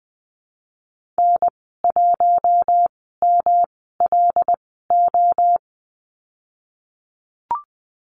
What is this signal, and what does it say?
Morse code, heard as a keyed tone, about 20 wpm, spaced as for 14 wpm, 700 Hz. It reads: N1MLO
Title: 147 Lesson 24g - Letter y - Call Signs 20wpm